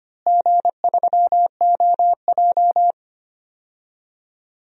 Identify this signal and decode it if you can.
Morse code, heard as a keyed tone, about 25 wpm, 700 Hz. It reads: G3OJ
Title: DXCC-100 Call Signs - 4 Characters - Repeated 2x - 25wpm